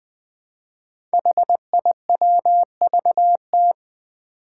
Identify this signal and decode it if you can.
Morse code, heard as a keyed tone, about 20 wpm, 700 Hz. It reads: HIWVT
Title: Five Challenging Letter-Number Combinations 20wpm